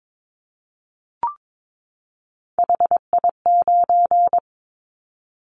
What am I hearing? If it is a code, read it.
Morse code, heard as a keyed tone, about 22 wpm, 700 Hz. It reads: HI9
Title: Callsign Prefixes for Top 100 DXCC - 22wpm